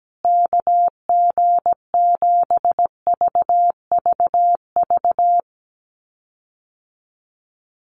Morse code, heard as a keyed tone, about 17 wpm, 700 Hz. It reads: KG7VVV
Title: USA Call Signs Repeated 3x 17wpm